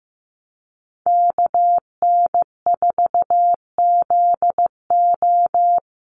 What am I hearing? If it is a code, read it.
Morse code, heard as a keyed tone, about 15 wpm, 700 Hz. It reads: KN4ZO
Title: USA Call Signs Repeated 2x 15wpm